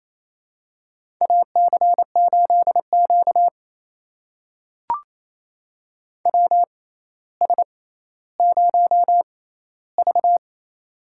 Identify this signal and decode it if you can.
Morse code, heard as a keyed tone, about 28 wpm, spaced as for 14 wpm, 700 Hz. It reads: AC8Q WS0V
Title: USA Call Signs - 4 Characters - 6x Character Spacing - 28wpm